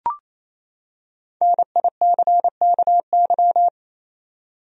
Morse code, heard as a keyed tone, about 28 wpm, 700 Hz. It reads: NICKY
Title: USA Common Names - Max 5 Letters - 28wpm